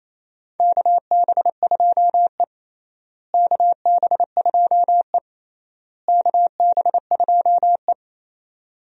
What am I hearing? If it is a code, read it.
Morse code, heard as a keyed tone, about 28 wpm, 700 Hz. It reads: KB2E KB2E KB2E
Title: USA Call Signs - 4 Characters - Repeated 3x - 28wpm